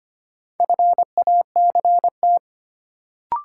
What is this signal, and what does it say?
Morse code, heard as a keyed tone, about 25 wpm, 700 Hz. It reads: FACT
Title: Sentences from Top 500 Words - New Word Drill - 25wpm